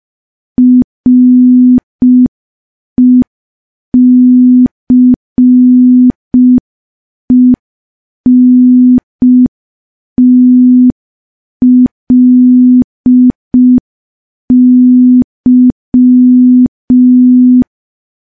Morse code, heard as a keyed tone, about 5 wpm, 250 Hz. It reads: RECENTLY
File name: morse / morse-training-05WPM